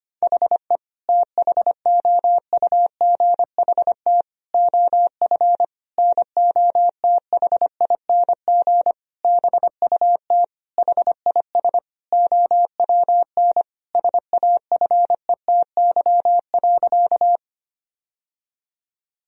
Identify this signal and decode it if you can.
Morse code, heard as a keyed tone, about 25 wpm, 700 Hz. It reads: HE THOUGHT OF NOTHING BUT HIS OWN SAFETY.
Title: Sentences from Top 1000 Words - Review Part 4 - 25wpm